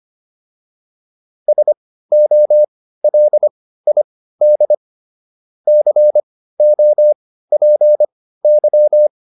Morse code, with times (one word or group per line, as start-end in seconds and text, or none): 1.5-4.7 SOLID
5.7-9.1 COPY